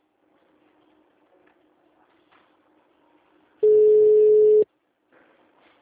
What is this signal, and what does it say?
Ringback tone (Europe, 425 Hz).